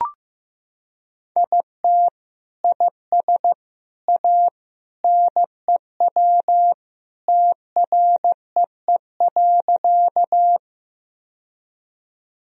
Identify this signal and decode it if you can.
Morse code, heard as a keyed tone, about 15 wpm, 700 Hz. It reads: IT IS A NEW TREE.
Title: Sentences from Top 1000 Words - Review Part 1 - 15wpm